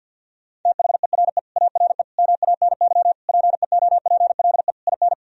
Morse code, heard as a keyed tone, about 51 wpm, 700 Hz. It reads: THERE ARE MANY PEOPLE IN
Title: Sentences from Top 1000 Words - Review Part 1 - 50wpm